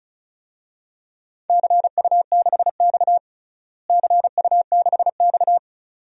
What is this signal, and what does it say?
Morse code, heard as a keyed tone, about 36 wpm, 700 Hz. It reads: CU6X CU6X
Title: DXCC-100 Call Signs - 4 Characters - Repeated 2x - 35wpm